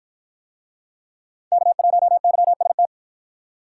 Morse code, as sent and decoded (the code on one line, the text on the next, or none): -.- .---- -.-- ... -
K1YST